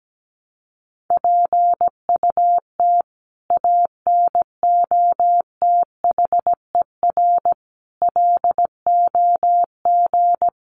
Morse code, heard as a keyed tone, about 17 wpm, 700 Hz. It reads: PUT ANOTHER LOG
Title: Sentences from Top 1400 Words 17wpm